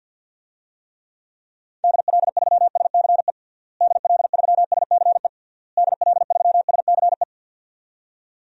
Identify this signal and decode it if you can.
Morse code, heard as a keyed tone, about 46 wpm, 700 Hz. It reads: DL3SCE DL3SCE DL3SCE
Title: DXCC-100 Call Signs - 6 Characters - Repeated 3x - 50wpm